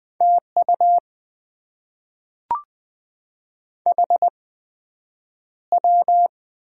Morse code, heard as a keyed tone, about 20 wpm, 700 Hz. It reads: T U H W
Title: Top 20 QSO Elements - 8x Character Spacing - 20wpm